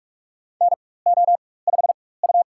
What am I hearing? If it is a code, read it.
Morse code, heard as a keyed tone, about 46 wpm, spaced as for 21 wpm, 700 Hz. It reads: NO5U
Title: USA Call Signs - 4 Characters - 4x Character Spacing - 45wpm